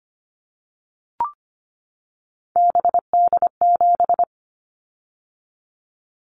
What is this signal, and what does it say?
Morse code, heard as a keyed tone, about 25 wpm, 700 Hz. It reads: BD7
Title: Callsign Prefixes for Top 100 DXCC - 25wpm